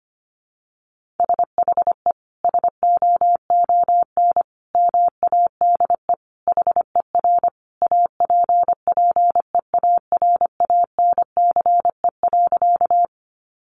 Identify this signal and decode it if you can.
Morse code, heard as a keyed tone, about 25 wpm, 700 Hz. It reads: SHE SOON MADE HER APPEARANCE.
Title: Sentences from Top 1100 Words 25wpm